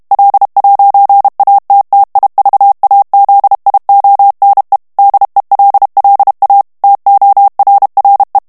Morse code, heard as a keyed tone, about 32 wpm, 800 Hz. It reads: L'ATTIVAZIONE DELLA TORRE
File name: ITALIEN texte clair 1 vitesse 30 mots minute